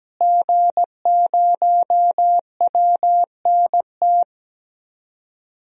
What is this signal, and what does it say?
Morse code, heard as a keyed tone, about 17 wpm, 700 Hz. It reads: G0WNT
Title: DXCC-100 Call Signs - 5 Characters - Repeated 2x - 17wpm